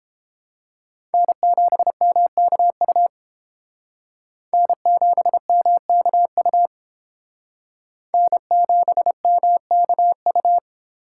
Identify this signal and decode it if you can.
Morse code, heard as a keyed tone, about 29 wpm, 700 Hz. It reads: N7MKU N7MKU N7MKU